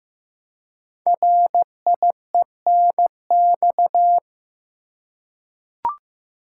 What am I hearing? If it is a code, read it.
Morse code, heard as a keyed tone, about 15 wpm, 700 Hz. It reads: RIENX
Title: Code Groups - 5 Letters - 15wpm